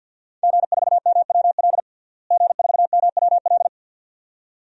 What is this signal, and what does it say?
Morse code, heard as a keyed tone, about 50 wpm, 700 Hz. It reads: G4MWL G4MWL
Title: DXCC-100 Call Signs - 5 Characters - Repeated 2x - 50wpm